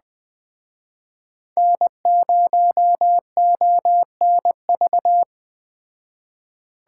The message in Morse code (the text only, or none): N0ONV